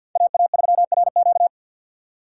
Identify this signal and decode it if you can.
Morse code, heard as a keyed tone, about 47 wpm, 700 Hz. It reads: AA3RQ
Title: USA Call Signs - 6x Character Spacing - 50wpm